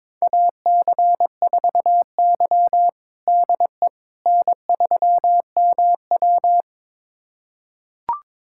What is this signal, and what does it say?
Morse code, heard as a keyed tone, about 22 wpm, 700 Hz. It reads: AC4Y DE N3MW
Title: USA Callsign DE Callsign - 4 Characters - 22wpm